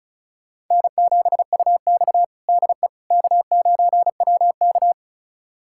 Morse code, heard as a keyed tone, about 36 wpm, 700 Hz. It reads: N7UX DE K9WK
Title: USA Callsign DE Callsign - 4 Characters - 35wpm